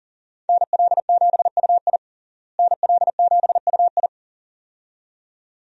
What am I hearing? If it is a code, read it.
Morse code, heard as a keyed tone, about 40 wpm, 700 Hz. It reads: NL7UI NL7UI
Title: USA Call Signs - 5 Characters - Repeated 2x - 40wpm